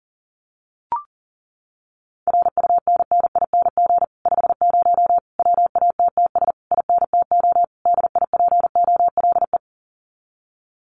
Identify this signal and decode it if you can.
Morse code, heard as a keyed tone, about 40 wpm, 700 Hz. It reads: RUNNING 50 WATTS INTO DIPOLE
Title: Common QSO Phrases Encore 40wpm